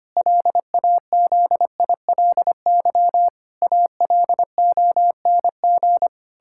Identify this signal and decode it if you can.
Morse code, heard as a keyed tone, about 25 wpm, 700 Hz. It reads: LAZILY ALONG
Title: Pangram Challenge v2 - 25wpm